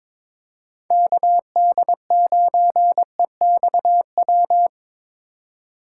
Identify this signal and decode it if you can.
Morse code, heard as a keyed tone, about 22 wpm, 700 Hz. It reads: KD9EXW
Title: USA Call Signs - 2x Character Spacing - 22wpm